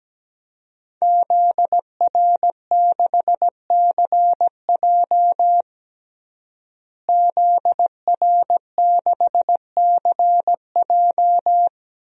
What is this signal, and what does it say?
Morse code, heard as a keyed tone, about 17 wpm, 700 Hz. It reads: ZR6CJ ZR6CJ
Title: DXCC-100 Call Signs - 5 Characters - Repeated 2x - 17wpm